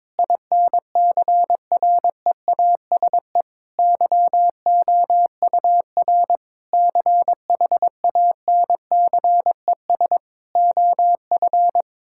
Morse code, heard as a keyed tone, about 22 wpm, 700 Hz. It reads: INCREASE YOUR CHANCES OF
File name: practice / 22wpm / Sentences with 8-Word Phrases - 22wpm